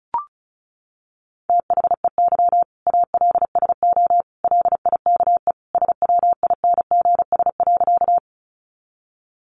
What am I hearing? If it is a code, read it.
Morse code, heard as a keyed tone, about 34 wpm, 700 Hz. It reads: THEY ALSO LIKE SWINGS.
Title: Easy Phrases 35wpm